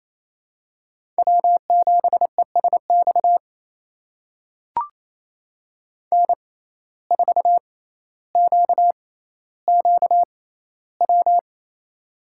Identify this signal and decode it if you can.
Morse code, heard as a keyed tone, about 28 wpm, spaced as for 14 wpm, 700 Hz. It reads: W7ESX N4QQW
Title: USA Call Signs - 5 Characters - 6x Character Spacing - 28wpm